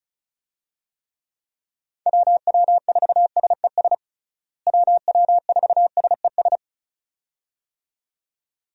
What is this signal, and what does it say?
Morse code, heard as a keyed tone, about 36 wpm, 700 Hz. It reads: WW4SES WW4SES
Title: USA Call Signs - 6 Characters - Repeated 2x - 35wpm